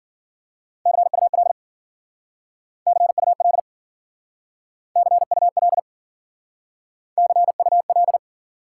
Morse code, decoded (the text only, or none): CUL CUL CUL CUL